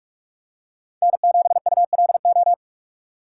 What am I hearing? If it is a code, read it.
Morse code, heard as a keyed tone, about 42 wpm, 700 Hz. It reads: N7ULO
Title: USA Call Signs - 5 Characters - 6x Character Spacing - 45wpm